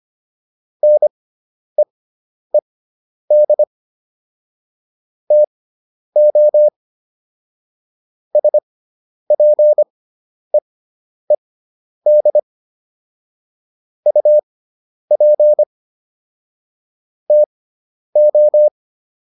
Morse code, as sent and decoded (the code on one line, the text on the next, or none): -. . -.. / - --- / ... .--. . -.. / ..- .--. / - ---
NEED TO SPEED UP TO